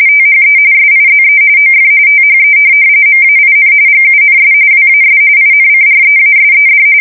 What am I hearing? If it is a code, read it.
RTTY, about 45.45 baud. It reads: THE SIZE OF THE MAIN GUNS WAS INCREASED
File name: fsk170 / USS-New-Jersey